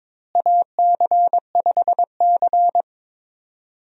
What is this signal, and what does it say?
Morse code, heard as a keyed tone, about 22 wpm, 700 Hz. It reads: AC5C
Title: USA Call Signs - 4 Characters - Repeated 2x - 22wpm